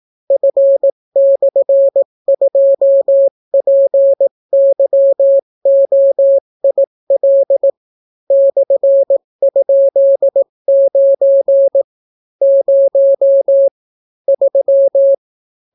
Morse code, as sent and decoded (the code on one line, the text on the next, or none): ..-. -..-. ..--- .--. -.-- --- .. .-.. / -..-. ..--.. ----. / ----- / ...--
F/2PYOIL /?9 0 3